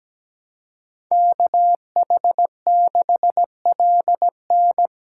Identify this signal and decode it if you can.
Morse code, heard as a keyed tone, about 20 wpm, 700 Hz. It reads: KH6LN
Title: Speed Racing - USA 2x2 Call Signs 17wpm